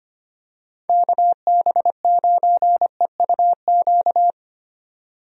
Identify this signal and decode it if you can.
Morse code, heard as a keyed tone, about 25 wpm, 700 Hz. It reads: KB9EUQ
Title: USA Call Signs - 6 Characters - 25wpm